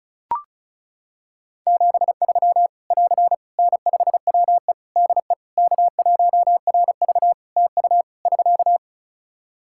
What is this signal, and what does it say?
Morse code, heard as a keyed tone, about 36 wpm, 700 Hz. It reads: 73 <AR> N5WE DE K1RV TU <SK>
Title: Practice QSO - Exchange 4 - Ending - 35wpm